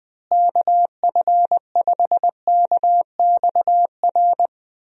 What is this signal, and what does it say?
Morse code, heard as a keyed tone, about 20 wpm, 700 Hz. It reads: KF5KXR
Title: USA Call Signs - 6 Characters - Repeated 3x - 20wpm